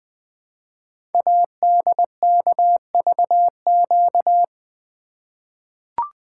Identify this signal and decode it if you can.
Morse code, heard as a keyed tone, about 20 wpm, 700 Hz. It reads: ADKVQ
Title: Code Groups - 5 Letters - 20wpm